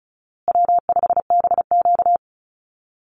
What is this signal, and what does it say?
Morse code, heard as a keyed tone, about 34 wpm, 700 Hz. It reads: W5BQ
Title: Easy Call Signs Repeated 3x - 35wpm